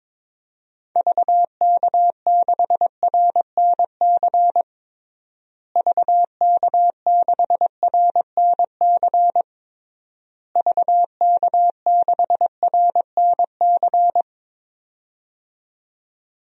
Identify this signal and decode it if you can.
Morse code, heard as a keyed tone, about 22 wpm, 700 Hz. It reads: VK6RNC VK6RNC VK6RNC
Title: DXCC-100 Call Signs - 6 Characters - Repeated 3x - 22wpm